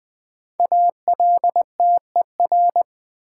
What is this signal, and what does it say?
Morse code, heard as a keyed tone, about 20 wpm, 700 Hz. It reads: ALTER
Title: 5 Letter Words - First 3 Letters Given - 20wpm